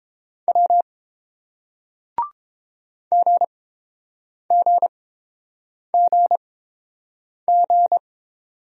Morse code, heard as a keyed tone, about 30 wpm, spaced as for 7 wpm, 700 Hz. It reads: W GGGG